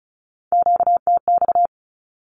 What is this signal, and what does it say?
Morse code, heard as a keyed tone, about 34 wpm, 700 Hz. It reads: QTX
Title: Q-Codes 35wpm